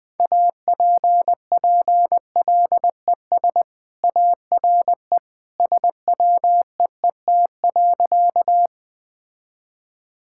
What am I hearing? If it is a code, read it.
Morse code, heard as a keyed tone, about 20 wpm, 700 Hz. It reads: APPLES ARE SWEET.
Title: Easy Sentences with Hints - 20wpm